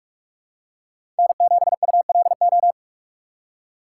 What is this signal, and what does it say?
Morse code, heard as a keyed tone, about 47 wpm, 700 Hz. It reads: N7ULO
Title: USA Call Signs - 5 Characters - 2x Character Spacing - 45wpm